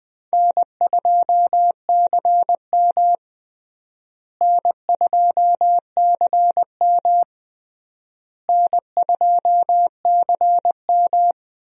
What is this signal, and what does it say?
Morse code, heard as a keyed tone, about 20 wpm, 700 Hz. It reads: N2CM N2CM N2CM
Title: USA Call Signs - 4 Characters - Repeated 3x - 20wpm